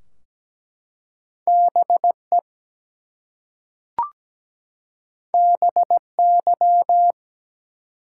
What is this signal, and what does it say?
Morse code, heard as a keyed tone, about 17 wpm, 700 Hz. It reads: BE BY
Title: Two Letter Words - 17wpm